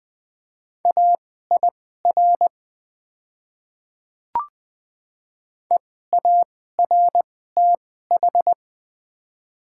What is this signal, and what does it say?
Morse code, heard as a keyed tone, about 20 wpm, spaced as for 14 wpm, 700 Hz. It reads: AIR EARTH
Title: 2x Character Spacing - Top 300 Words - 20wpm